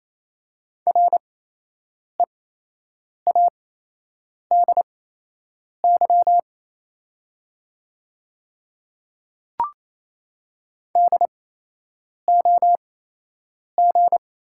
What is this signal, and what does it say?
Morse code, heard as a keyed tone, about 28 wpm, spaced as for 8 wpm, 700 Hz. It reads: READY DOG
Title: 8x Character Spacing - Top 500 Words - 28wpm